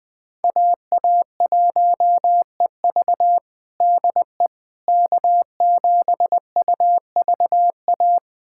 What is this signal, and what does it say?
Morse code, heard as a keyed tone, about 20 wpm, 700 Hz. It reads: AA1EV DE K7UVA
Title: USA Callsign DE Callsign - 5 Characters - 20wpm